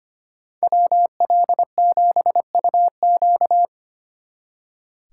Morse code, heard as a keyed tone, about 29 wpm, 700 Hz. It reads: WL7UQ
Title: Speed Racing - USA 2x2 Call Signs 25wpm